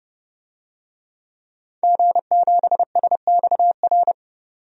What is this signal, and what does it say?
Morse code, heard as a keyed tone, about 30 wpm, 700 Hz. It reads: G7SXR
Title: DXCC-100 Call Signs - 5 Characters - 30wpm